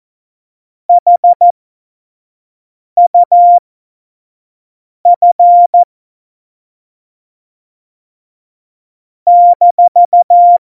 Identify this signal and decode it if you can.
Morse code, heard as a keyed tone, about 14 wpm, spaced as for 5 wpm, 700 Hz. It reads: HUF -
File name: Morsekurs / Witze / Witz 14. Januar 2020, 25 F70